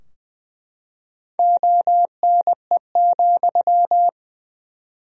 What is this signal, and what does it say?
Morse code, heard as a keyed tone, about 21 wpm, 700 Hz. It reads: ONE,
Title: 4x Word-Spacing - Sentences from Top 200 Words - 20wpm